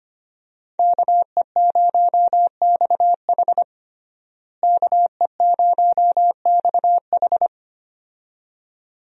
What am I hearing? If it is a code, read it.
Morse code, heard as a keyed tone, about 25 wpm, 700 Hz. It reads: KE0XH KE0XH
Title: USA Call Signs - 5 Characters - Repeated 2x - 25wpm